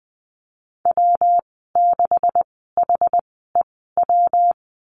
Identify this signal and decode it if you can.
Morse code, heard as a keyed tone, about 20 wpm, spaced as for 15 wpm, 700 Hz. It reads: W6HEW
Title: 140 Lesson 23g - Letter f - Call Signs 20wpm